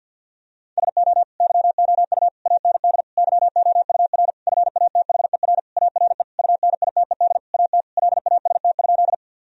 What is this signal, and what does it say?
Morse code, heard as a keyed tone, about 50 wpm, 700 Hz. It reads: SO YOU AND YOUR FATHER ARE UNITED AT LAST?
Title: Sentences from Top 1000 Words - Review Part 4 - 50wpm